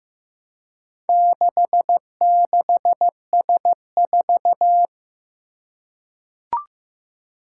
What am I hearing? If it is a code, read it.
Morse code, heard as a keyed tone, about 15 wpm, 700 Hz. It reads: 66S4
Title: Four Challenging Letter-Number Combinations 15wpm